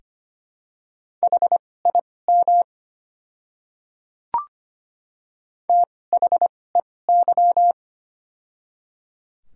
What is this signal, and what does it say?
Morse code, heard as a keyed tone, about 25 wpm, spaced as for 18 wpm, 700 Hz. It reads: HIM THEY
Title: 2x Character Spacing - Top 500 Words - 25wpm